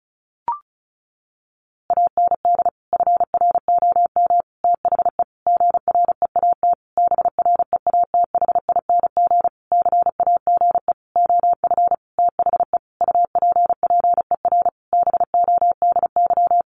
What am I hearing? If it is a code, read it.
Morse code, heard as a keyed tone, about 34 wpm, 700 Hz. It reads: AND FROM THE GREAT BREATHING CAGE OF THE UPPER BODY
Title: A Sound Of Thunder 35wpm